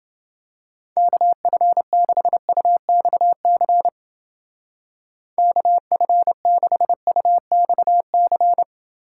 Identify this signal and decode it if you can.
Morse code, heard as a keyed tone, about 28 wpm, 700 Hz. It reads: KF6UXC KF6UXC